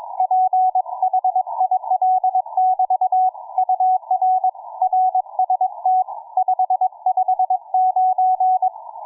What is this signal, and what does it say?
Morse code, heard as a keyed tone, about 22 wpm, 740 Hz. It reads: PHIL<BT> UR RST 559